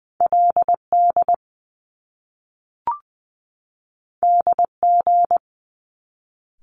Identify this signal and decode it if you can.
Morse code, heard as a keyed tone, about 20 wpm, 700 Hz. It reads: L D D G
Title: Top Two Letter Combinations in Words 20wpm